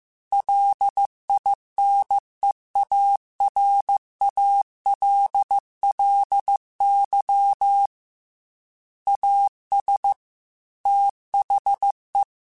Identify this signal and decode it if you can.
Morse code, heard as a keyed tone, about 15 wpm, 800 Hz. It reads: LINEARALLY AS THE
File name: Technical Subjects / Technical Subjects-mp3s / XL @13